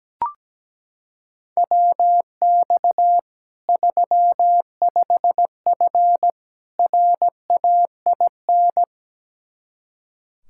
Morse code, heard as a keyed tone, about 17 wpm, 700 Hz. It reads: WX 35F RAIN
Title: K1USN SST QSO Practice 17wpm